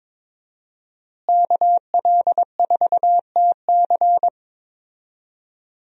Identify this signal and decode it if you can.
Morse code, heard as a keyed tone, about 23 wpm, 700 Hz. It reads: KL4TC
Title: USA Call Signs - 5 Characters - Repeated 3x - 22wpm